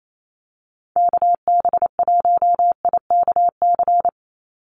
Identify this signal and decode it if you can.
Morse code, heard as a keyed tone, about 28 wpm, 700 Hz. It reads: KB1IKC
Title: USA Call Signs 28wpm